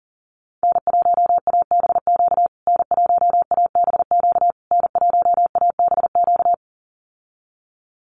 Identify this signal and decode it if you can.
Morse code, heard as a keyed tone, about 37 wpm, 700 Hz. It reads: N1ABQ N1ABQ N1ABQ
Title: USA Call Signs Repeated 3x 40wpm